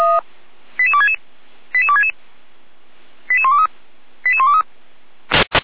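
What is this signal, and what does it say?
Five-tone selective call (ZVEI), heard as quick runs of five tones, tones of about 70 ms. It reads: 88266 88266 88123 88123
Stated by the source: ZVEI five-tone selective call received on an SDR by the Orlamünde fire brigade (feuerwehr-orlamuende).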